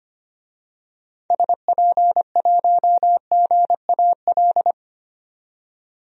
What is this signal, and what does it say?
Morse code, heard as a keyed tone, about 25 wpm, 700 Hz. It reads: SP1GAL
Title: DXCC-100 Call Signs - 6 Characters - 25wpm